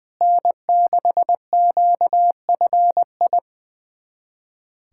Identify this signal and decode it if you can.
Morse code, heard as a keyed tone, about 20 wpm, 700 Hz. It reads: N6QFI
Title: USA Callsign DE Callsign - 5 Characters - 20wpm